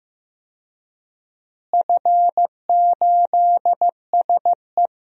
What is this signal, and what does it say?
Morse code, heard as a keyed tone, about 15 wpm, 700 Hz. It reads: F8SE
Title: DXCC-100 Call Signs - 4 Characters - Repeated 2x - 15wpm